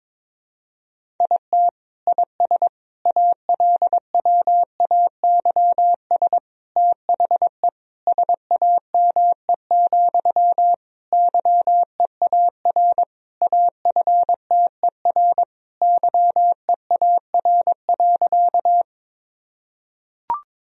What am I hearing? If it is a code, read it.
Morse code, heard as a keyed tone, about 22 wpm, 700 Hz. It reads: IT IS ALWAYS THE SAME, YEAR AFTER YEAR.
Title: Sentences from Top 1000 Words - Review Part 1 - 22wpm